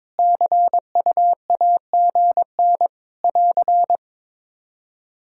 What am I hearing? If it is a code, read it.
Morse code, heard as a keyed tone, about 22 wpm, 700 Hz. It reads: CUAGN <AR>
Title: K1USN SST QSO Practice 22wpm